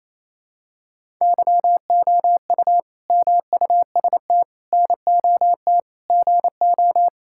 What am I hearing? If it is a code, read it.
Morse code, heard as a keyed tone, about 28 wpm, 700 Hz. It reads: YOU MUST NOT GO
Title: Sentences from Top 1000 Words - Review Part 5 - 28wpm